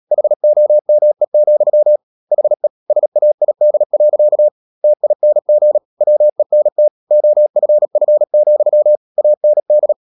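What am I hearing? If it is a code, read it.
Morse code, heard as a keyed tone, about 37 wpm, 600 Hz. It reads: HOME, HE SAID. TING WENT OFF, AND
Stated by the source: Morse code practice at 35–39 wpm sending English text of Ken Follett's novel Never, from CW med Gnister.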